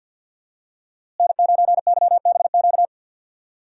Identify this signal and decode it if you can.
Morse code, heard as a keyed tone, about 47 wpm, 700 Hz. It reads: N9YBQ
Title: USA Call Signs - 5 Characters - Repeated 3x - 50wpm